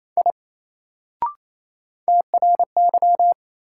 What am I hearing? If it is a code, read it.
Morse code, heard as a keyed tone, about 28 wpm, 700 Hz. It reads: I TRY
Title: Sentences from Top 200 Words - New Word Drill - 28wpm